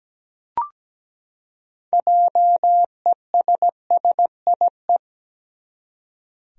Morse code, heard as a keyed tone, about 17 wpm, 700 Hz. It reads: JESSIE
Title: USA Common Names - 17wpm